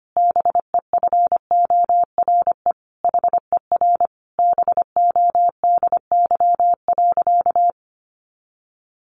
Morse code, heard as a keyed tone, about 25 wpm, 700 Hz. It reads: BEFORE HER BODY.
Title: Sentences from Top 1500 Words 25wpm